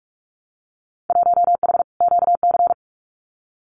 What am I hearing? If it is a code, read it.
Morse code, heard as a keyed tone, about 46 wpm, 700 Hz. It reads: 1H QC
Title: Winter Field Day Exchanges 45wpm